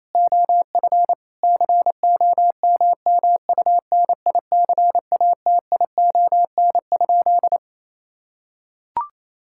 Morse code, heard as a keyed tone, about 27 wpm, 700 Hz. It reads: OF COMMUNICATION?
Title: Sentences from Top 1000 Words - Review Part 6 - 28wpm